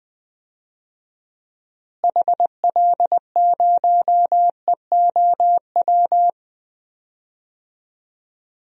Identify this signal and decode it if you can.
Morse code, heard as a keyed tone, about 20 wpm, 700 Hz. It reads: HL0EOW
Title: DXCC-100 Call Signs - 6 Characters - Repeated 2x - 20wpm